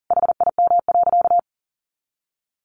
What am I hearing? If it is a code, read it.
Morse code, heard as a keyed tone, about 40 wpm, 700 Hz. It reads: HIM.
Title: Sentences from Top 1400 Words 40wpm